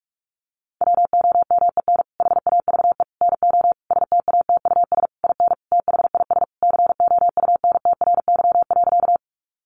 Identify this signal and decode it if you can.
Morse code, heard as a keyed tone, about 44 wpm, 700 Hz. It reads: WOMEN HAVE NO STATUS IN THIS COUNTRY.
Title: Sentences from Top 1400 Words 45wpm